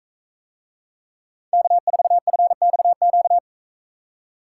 Morse code, heard as a keyed tone, about 44 wpm, 700 Hz. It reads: K4FXQ